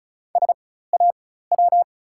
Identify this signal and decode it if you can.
Morse code, heard as a keyed tone, about 36 wpm, spaced as for 16 wpm, 700 Hz. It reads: SAW
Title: 4x Character Spacing - Top 500 Words - 35wpm